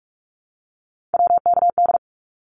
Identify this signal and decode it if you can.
Morse code, heard as a keyed tone, about 44 wpm, 700 Hz. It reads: WKD
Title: Abbreviations 45wpm